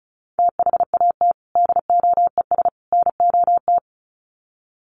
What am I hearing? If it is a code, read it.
Morse code, heard as a keyed tone, about 34 wpm, 700 Hz. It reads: THAT DOES NOT
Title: Sets of 3 Words 35wpm